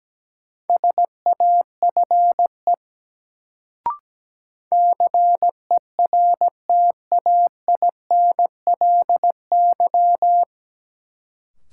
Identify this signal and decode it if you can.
Morse code, heard as a keyed tone, about 17 wpm, 700 Hz. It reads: SAFE CERTAINLY
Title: Sentences from Top 900 Words - New Word Drill - 17wpm